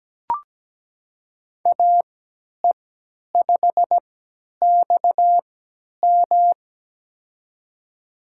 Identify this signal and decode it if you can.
Morse code, heard as a keyed tone, about 17 wpm, spaced as for 10 wpm, 700 Hz. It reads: AE5XM
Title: USA Call Signs - 5 Characters - 3x Character Spacing - 17wpm